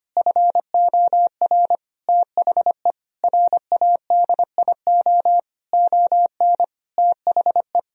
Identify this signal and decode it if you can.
Morse code, heard as a keyed tone, about 25 wpm, 700 Hz. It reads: FOR THE RADIO ON THE
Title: Sentences from Top 1000 Words - Review Part 3 - 25wpm